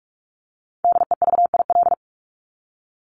Morse code, heard as a keyed tone, about 44 wpm, 700 Hz. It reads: DEVIL
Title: Top 1751-2000 Words 45wpm